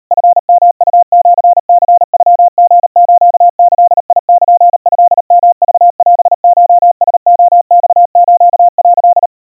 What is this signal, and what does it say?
Morse code (Japanese Wabun code), heard as a keyed tone, about 39 wpm, 700 Hz. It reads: チヨウシニノリスギルトヨクオコラレマス」
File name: kawa / Morse sample / kamyu 140 cpm 700 Hz Crispy